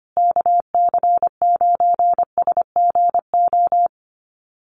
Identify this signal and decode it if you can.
Morse code, heard as a keyed tone, about 25 wpm, 700 Hz. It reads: KC9SGO
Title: USA Call Signs Repeated 3x 25wpm